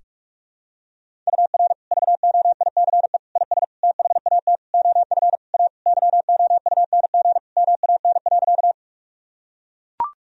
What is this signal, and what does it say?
Morse code, heard as a keyed tone, about 43 wpm, 700 Hz. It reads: UR VOICE IS THAT OF A YOUNG MAN.
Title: Sentences from Top 500 Words and Abbreviations 45wpm